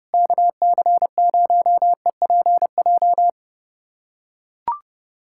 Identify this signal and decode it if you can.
Morse code, heard as a keyed tone, about 30 wpm, 700 Hz. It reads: KC0EPJ
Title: USA Callsign DE Callsign 30wpm